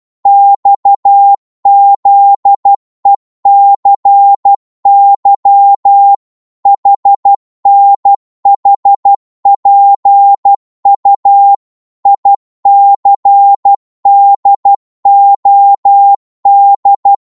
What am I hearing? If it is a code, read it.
Morse code, heard as a keyed tone, about 12 wpm, 800 Hz. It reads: XZECY HNHPU ICDOD
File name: Morse / ltr-12wpm